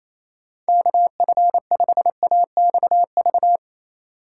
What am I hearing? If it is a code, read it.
Morse code, heard as a keyed tone, about 28 wpm, 700 Hz. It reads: KF5AXV
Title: USA Call Signs - 6 Characters - 2x Character Spacing - 28wpm